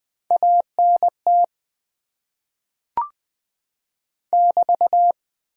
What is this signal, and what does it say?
Morse code, heard as a keyed tone, about 20 wpm, 700 Hz. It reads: ANT <BT>
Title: Top 90 QSO Elements - 8x Character Spacing - 20wpm